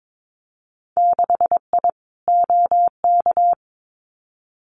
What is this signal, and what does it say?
Morse code, heard as a keyed tone, about 22 wpm, 700 Hz. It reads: 6I OK
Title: Winter Field Day Exchanges Encore 22wpm